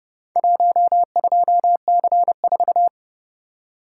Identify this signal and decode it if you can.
Morse code, heard as a keyed tone, about 30 wpm, 700 Hz. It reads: 12C4